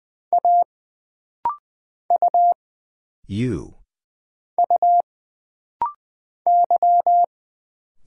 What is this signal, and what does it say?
Morse code, heard as a keyed tone, about 20 wpm, 700 Hz. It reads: A U U Y
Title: Single Letter-Number Alt - Rapid-Fire 20wpm